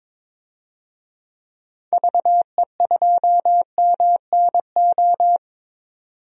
Morse code, heard as a keyed tone, about 22 wpm, 700 Hz. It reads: VE2MNO
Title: DXCC-100 Call Signs - 6 Characters - 22wpm